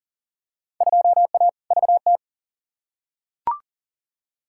Morse code, heard as a keyed tone, about 40 wpm, 700 Hz. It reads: 2A VT
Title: Field Day Exchanges Encore - 40wpm